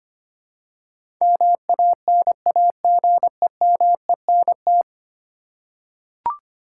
Morse code, heard as a keyed tone, about 25 wpm, 700 Hz. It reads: MANAGEMENT
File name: 10 Letter Words - First 3 Letters Given - 25wpm